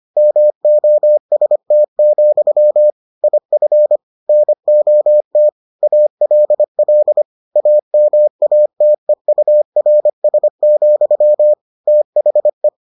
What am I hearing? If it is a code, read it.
Morse code, heard as a keyed tone, about 25 wpm, 600 Hz. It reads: MOST, IF NOT ALL AMATEURS, THE